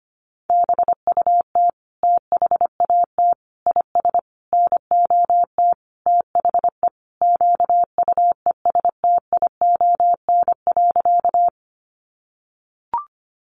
Morse code, heard as a keyed tone, about 25 wpm, 700 Hz. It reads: BUT THAT IS NOT THE QUESTION.
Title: Sentences from Top 600 Words 25wpm